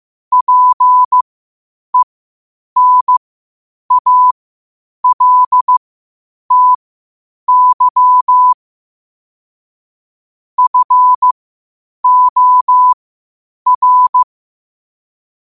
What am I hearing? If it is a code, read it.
Morse code, heard as a keyed tone, about 15 wpm, spaced as for 8 wpm, 1000 Hz. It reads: PENALTY FOR
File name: qotdfiles / 8wpm